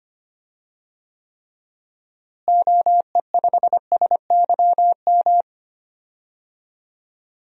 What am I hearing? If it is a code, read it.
Morse code, heard as a keyed tone, about 25 wpm, 700 Hz. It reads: OE5SYM
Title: DXCC-100 Call Signs - 6 Characters - Repeated 2x - 25wpm